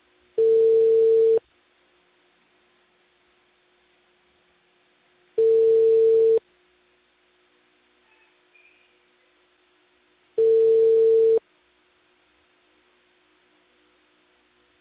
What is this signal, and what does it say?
Ringback tone (China, 450 Hz).